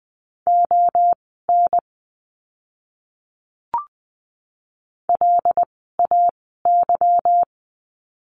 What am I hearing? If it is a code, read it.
Morse code, heard as a keyed tone, about 20 wpm, spaced as for 7 wpm, 700 Hz. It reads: ONLAY
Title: 146 Lesson 24f - Letter y - Words 20wpm